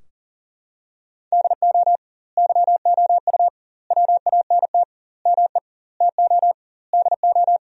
Morse code, heard as a keyed tone, about 37 wpm, 700 Hz. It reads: DO YOU WANT ME TO DO
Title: 2x Word-Spacing - Sentences from Top 200 Words - 40wpm